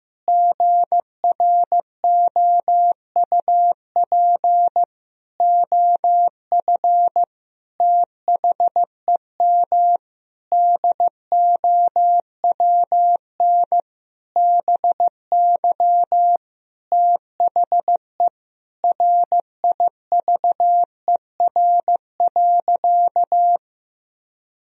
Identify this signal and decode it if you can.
Morse code, heard as a keyed tone, about 15 wpm, 700 Hz. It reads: GROUP OF THEM DOWN BY THE RIVER.
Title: Sentences from Top 1000 Words - Review Part 1 - 15wpm